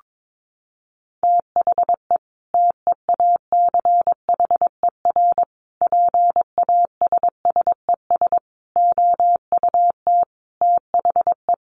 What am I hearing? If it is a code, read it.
Morse code, heard as a keyed tone, about 22 wpm, 700 Hz. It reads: THE TEACHER PASSES OUT THE
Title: Additional Easy Phrases 22wpm